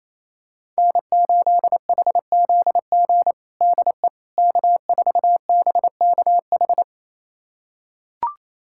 Morse code, heard as a keyed tone, about 28 wpm, 700 Hz. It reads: N8HZG DE K4BKH
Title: USA Callsign DE Callsign 28wpm